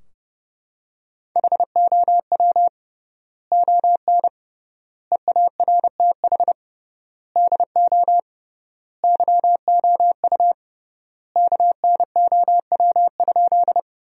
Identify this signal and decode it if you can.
Morse code, heard as a keyed tone, about 32 wpm, 700 Hz. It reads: HOW ON EARTH DO YOU KNOW?
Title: 3x Word-Spacing - Sentences from Top 200 Words - 30wpm